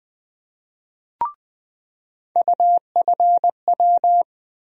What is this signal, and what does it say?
Morse code, heard as a keyed tone, about 20 wpm, 700 Hz. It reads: UFW
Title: USA Call Sign Postfixes - 3 Letters 20wpm